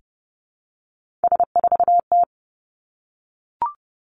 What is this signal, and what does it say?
Morse code, heard as a keyed tone, about 30 wpm, 700 Hz. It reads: S4T
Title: Three Challenging Letter-Number Combinations 30wpm